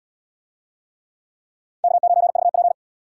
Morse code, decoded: K9VP

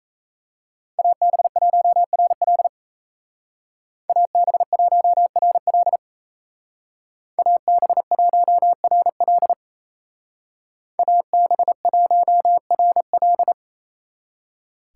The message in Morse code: AB1RL AB1RL AB1RL AB1RL

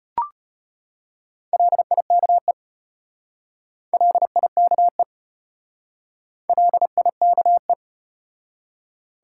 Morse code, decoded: LIKE LIKE LIKE

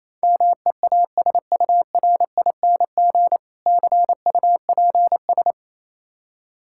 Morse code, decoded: MEASURING CUPS

